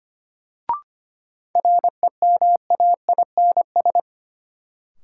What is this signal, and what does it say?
Morse code, heard as a keyed tone, about 25 wpm, 700 Hz. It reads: REMAINS
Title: Sets of 5 Words Encore - New Word Drill - 25wpm